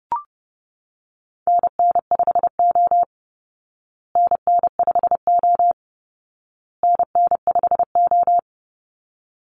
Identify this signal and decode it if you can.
Morse code, heard as a keyed tone, about 30 wpm, 700 Hz. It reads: NN5O NN5O NN5O
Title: Call Signs Repeated 2x - 30wpm